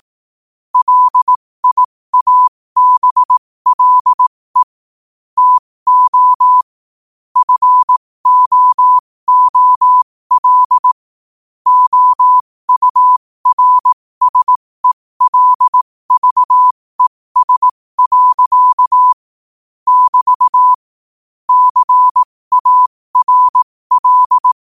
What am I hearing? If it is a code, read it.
Morse code, heard as a keyed tone, about 18 wpm, spaced as for 15 wpm, 1000 Hz. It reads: LIABLE TO FOOL OURSELVES. <BT> CARL
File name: qotdfiles / 15wpm